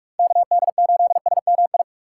Morse code, heard as a keyed tone, about 43 wpm, 700 Hz. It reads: KD8SMI
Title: USA Call Signs - 6 Characters - 6x Character Spacing - 45wpm